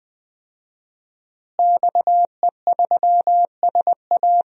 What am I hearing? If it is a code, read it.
Morse code, heard as a keyed tone, about 20 wpm, 700 Hz. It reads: XE3SA
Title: DXCC-100 Call Signs - 5 Characters - Repeated 2x - 20wpm